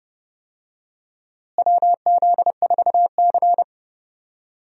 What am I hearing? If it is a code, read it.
Morse code, heard as a keyed tone, about 31 wpm, 700 Hz. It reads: WZ4C
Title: USA Call Signs - 4 Characters - Repeated 2x - 30wpm